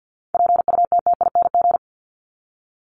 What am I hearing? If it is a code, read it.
Morse code, heard as a keyed tone, about 50 wpm, 700 Hz. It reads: PUTTING
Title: ING Suffix - Top 100 Words with Base Words 50wpm